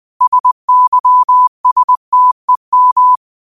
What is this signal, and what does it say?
Morse code, heard as a keyed tone, about 20 wpm, 1000 Hz. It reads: SYSTEM
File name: qotdfiles / 20wpm